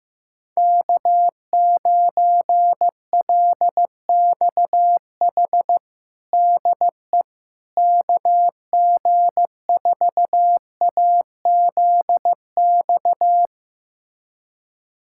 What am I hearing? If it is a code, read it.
Morse code, heard as a keyed tone, about 15 wpm, 700 Hz. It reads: K9LXH DE KG4AZX
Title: USA Callsign DE Callsign 15wpm